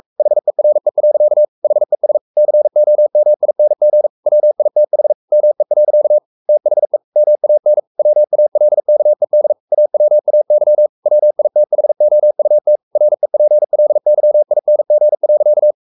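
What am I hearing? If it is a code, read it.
Morse code, heard as a keyed tone, about 44 wpm, 600 Hz. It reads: HERE. HES COMING WITH ME. THE MAN WALKED AWAY WITHOUT REPLYING.
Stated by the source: Morse code at 43 wpm sending Ken Follett's novel Never in English, from CW med Gnister.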